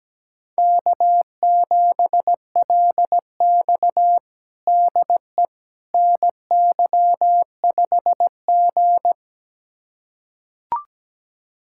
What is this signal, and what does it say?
Morse code, heard as a keyed tone, about 17 wpm, 700 Hz. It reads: K7LX DE NY5G
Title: USA Callsign DE Callsign - 4 Characters - 17wpm